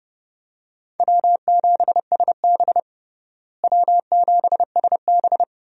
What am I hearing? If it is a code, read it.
Morse code, heard as a keyed tone, about 30 wpm, 700 Hz. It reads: W7SB W7SB
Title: USA Call Signs - 4 Characters - Repeated 2x - 30wpm